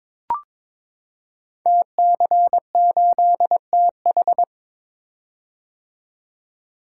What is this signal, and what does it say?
Morse code, heard as a keyed tone, about 23 wpm, 700 Hz. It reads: TC8TH
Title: Code Groups - 5 Letter:Numbers - 22wpm